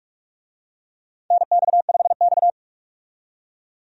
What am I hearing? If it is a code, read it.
Morse code, heard as a keyed tone, about 46 wpm, 700 Hz. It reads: NX5X
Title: USA Call Signs - 4 Characters - 45wpm